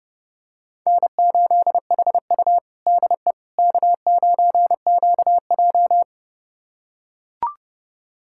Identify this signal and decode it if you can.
Morse code, heard as a keyed tone, about 30 wpm, 700 Hz. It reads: N8HU DE K9QJ
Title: USA Callsign DE Callsign - 4 Characters - 30wpm